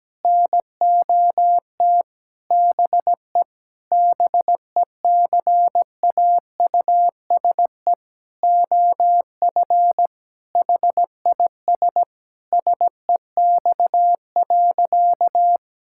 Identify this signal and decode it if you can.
Morse code, heard as a keyed tone, about 17 wpm, 700 Hz. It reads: NOT BE BECAUSE OF HIS SEX.
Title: Sentences from Top 1600 Words 17wpm